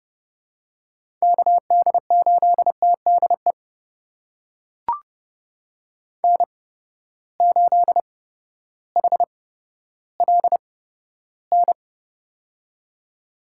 Morse code, heard as a keyed tone, about 30 wpm, spaced as for 14 wpm, 700 Hz. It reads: KD8TDE N8HLN
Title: USA Call Signs - 8x Character Spacing - 30wpm